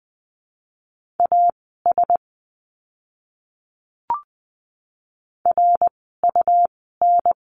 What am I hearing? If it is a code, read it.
Morse code, heard as a keyed tone, about 20 wpm, spaced as for 15 wpm, 700 Hz. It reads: AS RUN
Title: 101 Lesson 17f - Letter u - Words 20wpm